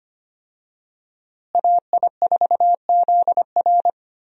Morse code, heard as a keyed tone, about 25 wpm, 700 Hz. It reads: AI4ZR
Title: USA Call Signs - 5 Characters - Repeated 3x - 25wpm